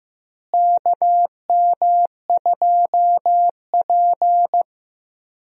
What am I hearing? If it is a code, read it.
Morse code, heard as a keyed tone, about 16 wpm, 700 Hz. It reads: KM2P
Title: USA Call Signs - 4 Characters - Repeated 3x - 15wpm